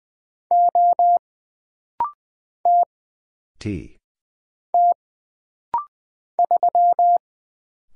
Morse code, heard as a keyed tone, about 20 wpm, 700 Hz. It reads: O T T 3
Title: Single Letter-Number Alt - Rapid-Fire 20wpm